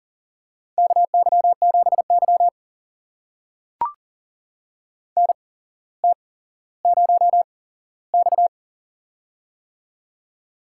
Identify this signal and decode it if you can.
Morse code, heard as a keyed tone, about 40 wpm, spaced as for 17 wpm, 700 Hz. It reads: KY7Y NT0X
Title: USA Call Signs - 8x Character Spacing - 40wpm